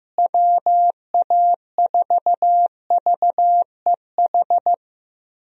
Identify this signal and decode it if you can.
Morse code, heard as a keyed tone, about 15 wpm, 700 Hz. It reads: WA4VEH
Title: USA Call Signs - 6 Characters - Repeated 2x - 15wpm